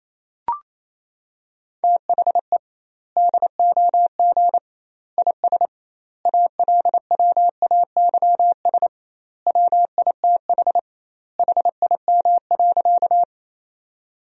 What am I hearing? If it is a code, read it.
Morse code, heard as a keyed tone, about 28 wpm, 700 Hz. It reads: THE DOG IS ALWAYS WITH HIM.
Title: 2x Word-Spacing - Sentences from Top 400 Words - 28wpm